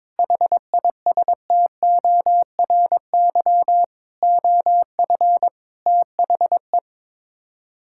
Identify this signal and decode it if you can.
Morse code, heard as a keyed tone, about 22 wpm, 700 Hz. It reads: HISTORY OF THE
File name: practice / 22wpm / Build A Phrase - 5-Word Phrases - 22wpm